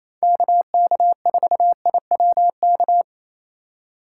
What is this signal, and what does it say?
Morse code, heard as a keyed tone, about 34 wpm, 700 Hz. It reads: KK4IWK
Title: Speed Racing - USA 2x3 Call Signs 28wpm